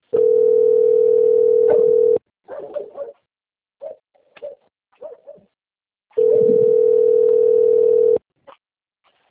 Ringback tone (North America), 440 + 480 Hz.
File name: out- - - - - .41263